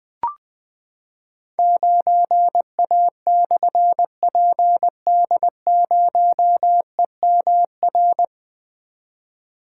Morse code, heard as a keyed tone, about 20 wpm, 700 Hz. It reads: 9A/PD0EMR
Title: Difficult Call Signs 20wpm